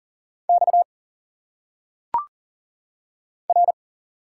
Morse code, heard as a keyed tone, about 40 wpm, 700 Hz. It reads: X R
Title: Single Letters - Phonetic Pronunciation 40wpm